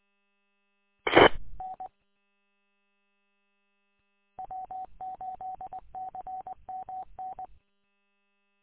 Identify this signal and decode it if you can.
Morse code, heard as a keyed tone, about 20 wpm, 750 Hz. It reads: TN W8CMN
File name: w8cmn- -combined